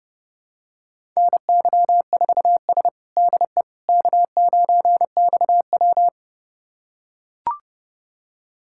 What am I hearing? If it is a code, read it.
Morse code, heard as a keyed tone, about 30 wpm, 700 Hz. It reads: NY4S DE K9XW
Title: USA Callsign DE Callsign - 4 Characters - 30wpm